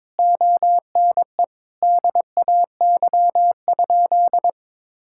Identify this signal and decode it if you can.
Morse code, heard as a keyed tone, about 22 wpm, 700 Hz. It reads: ONE DAY?
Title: Sentences from Top 1000 Words - Review Part 6 - 22wpm